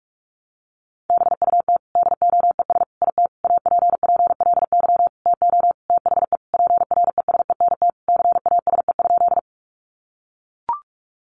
Morse code, heard as a keyed tone, about 44 wpm, 700 Hz. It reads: BUT DOES IT APPLY TO THE PRESENT CASE?
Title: Sentences from Top 1200 Words 45wpm